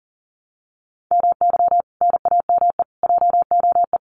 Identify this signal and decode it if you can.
Morse code, heard as a keyed tone, about 40 wpm, 700 Hz. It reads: MY NAME JOE
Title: Common QSO Phrases 40wpm